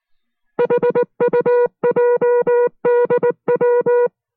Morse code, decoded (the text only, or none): HUJDW